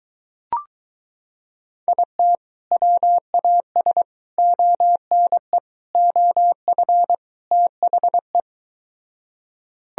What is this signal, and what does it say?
Morse code, heard as a keyed tone, about 23 wpm, 700 Hz. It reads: IT WAS ONE OF THE